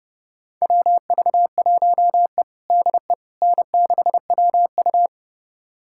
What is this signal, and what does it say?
Morse code, heard as a keyed tone, about 30 wpm, 700 Hz. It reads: WV1E DE N6WU
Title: USA Callsign DE Callsign - 4 Characters - 30wpm